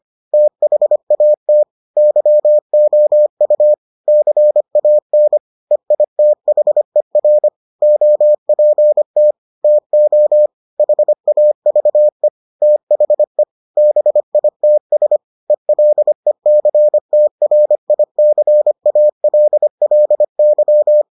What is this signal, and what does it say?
Morse code, heard as a keyed tone, about 25 wpm, 600 Hz. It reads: THAT YOU CAN EITHER OPT TO HAVE THE BITS ELECTRICALLY